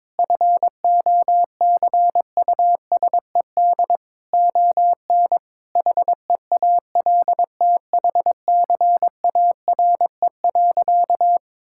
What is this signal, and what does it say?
Morse code, heard as a keyed tone, about 22 wpm, 700 Hz. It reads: FOCUSED ON HEALTHCARE.
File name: Sentences with 7-Word Phrases - 22wpm